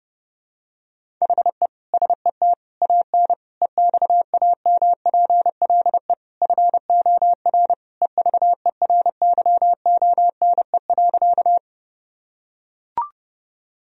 Morse code, heard as a keyed tone, about 30 wpm, 700 Hz. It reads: HE SET AN EXAMPLE FOR EVERYONE.
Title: Sentences from Top 1000 Words - Review Part 1 - 30wpm